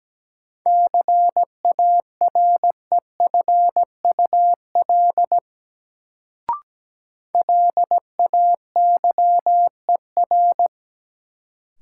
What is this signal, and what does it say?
Morse code, heard as a keyed tone, about 17 wpm, 700 Hz. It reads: CAREFUL LAYER
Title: Sentences from Top 1600 Words - New Word Drill Alt - 17wpm